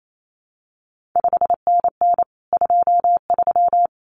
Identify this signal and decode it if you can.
Morse code, heard as a keyed tone, about 28 wpm, 700 Hz. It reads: 5NN 23
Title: CQ WW DX Contest Exchange Encore - 28wpm